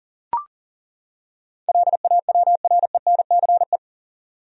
Morse code, heard as a keyed tone, about 40 wpm, 700 Hz. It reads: LAWRENCE
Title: USA Common Names - 40wpm